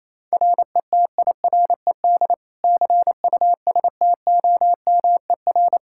Morse code, heard as a keyed tone, about 28 wpm, 700 Hz. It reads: RETIRED CUSTOMER
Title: Practice QSO - Exchange 3 - More Info - 28wpm